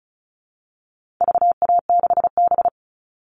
Morse code, heard as a keyed tone, about 34 wpm, 700 Hz. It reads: VA6B
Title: All Cut Numbers - 4-Digit 35wpm